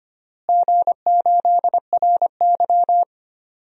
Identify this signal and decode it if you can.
Morse code, heard as a keyed tone, about 25 wpm, 700 Hz. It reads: G8RY
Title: DXCC-100 Call Signs - 4 Characters - Repeated 3x - 25wpm